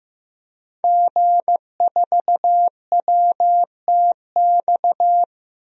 Morse code, heard as a keyed tone, about 15 wpm, 700 Hz. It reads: G4WTX
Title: DXCC-100 Call Signs - 5 Characters - Repeated 3x - 15wpm